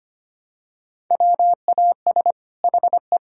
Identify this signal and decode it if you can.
Morse code, heard as a keyed tone, about 25 wpm, 700 Hz. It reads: WAS HE
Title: Sentences from Top 1000 Words - Review Part 4 - 25wpm